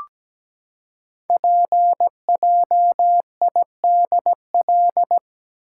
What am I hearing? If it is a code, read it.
Morse code, heard as a keyed tone, about 17 wpm, 700 Hz. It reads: PJIDL
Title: Code Groups - 5 Letters - 17wpm